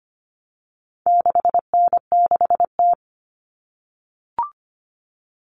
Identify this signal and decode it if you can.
Morse code, heard as a keyed tone, about 25 wpm, 700 Hz. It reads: 6N6T
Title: All Cut Numbers - 4-Digit 25wpm